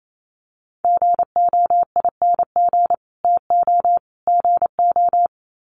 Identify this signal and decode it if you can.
Morse code, heard as a keyed tone, about 28 wpm, 700 Hz. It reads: GOING TO GO
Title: Sets of 3 Words 28wpm